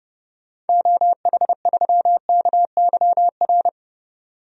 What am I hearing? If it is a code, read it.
Morse code, heard as a keyed tone, about 30 wpm, 700 Hz. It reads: OH3KYR
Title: DXCC-100 Call Signs - 6 Characters - 30wpm